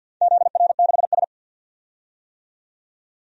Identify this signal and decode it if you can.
Morse code, heard as a keyed tone, about 49 wpm, 700 Hz. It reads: ZR6S
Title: DXCC-100 Call Signs - 4 Characters - 50wpm